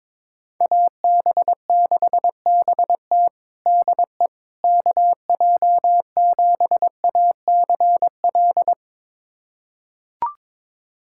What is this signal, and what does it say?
Morse code, heard as a keyed tone, about 22 wpm, 700 Hz. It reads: AB6BT DE KJ7ACL
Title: USA Callsign DE Callsign 22wpm